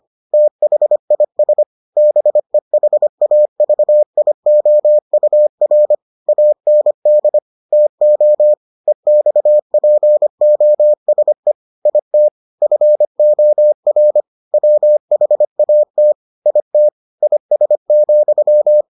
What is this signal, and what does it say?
Morse code, heard as a keyed tone, about 25 wpm, 600 Hz. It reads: THIS BEHAVIOUR AND TO EXPOSE IT FOR WHAT IT IS,